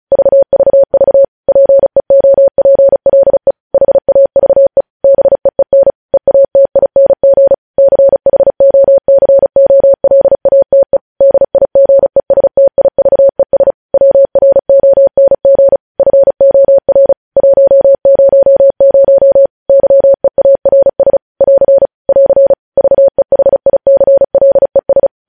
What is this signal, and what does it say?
Morse code, heard as a keyed tone, about 36 wpm, 550 Hz. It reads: VVV PEOPLE HAVE BEEN EATING CHOCOLATE DIGESTIVES WRONG FOR 100 YEARS <AR> <AR> VEHICLES
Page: news headlines in morse code at 35 wpm